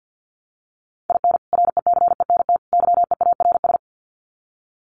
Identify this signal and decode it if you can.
Morse code, heard as a keyed tone, about 50 wpm, 700 Hz. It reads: IN RECENT YEARS
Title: Sets of 3 Words 50wpm